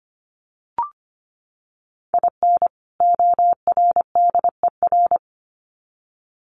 Morse code, heard as a keyed tone, about 25 wpm, 700 Hz. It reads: IN ORDER
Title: Sets of 2 Words 25wpm